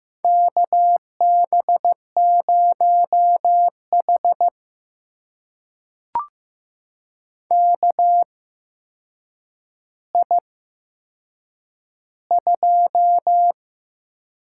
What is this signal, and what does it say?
Morse code, heard as a keyed tone, about 15 wpm, spaced as for 7 wpm, 700 Hz. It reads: KB0H KI2
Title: USA Call Signs - 4 Characters - 8x Character Spacing - 15wpm